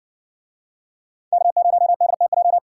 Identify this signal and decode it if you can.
Morse code, heard as a keyed tone, about 58 wpm, 700 Hz. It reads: K0DTJ